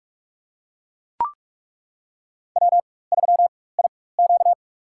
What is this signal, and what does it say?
Morse code, heard as a keyed tone, about 42 wpm, spaced as for 21 wpm, 700 Hz. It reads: W3IQ
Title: USA Call Signs - 4 Characters - 4x Character Spacing - 45wpm